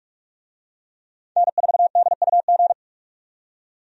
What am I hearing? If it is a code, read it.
Morse code, heard as a keyed tone, about 47 wpm, 700 Hz. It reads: N4DUG
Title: USA Call Signs - 5 Characters - 45wpm